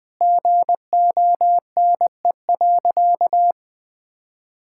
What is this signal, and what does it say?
Morse code, heard as a keyed tone, about 21 wpm, 700 Hz. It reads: GONE.
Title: Sentences with 3-Word Phrases - Spoken Beginning - 20wpm